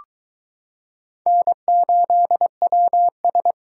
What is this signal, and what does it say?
Morse code, heard as a keyed tone, about 22 wpm, 700 Hz. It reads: N8WS